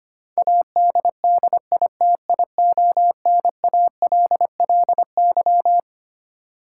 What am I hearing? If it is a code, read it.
Morse code, heard as a keyed tone, about 25 wpm, 700 Hz. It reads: ADDITIONALLY
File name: Twelve Letter Words - 25wpm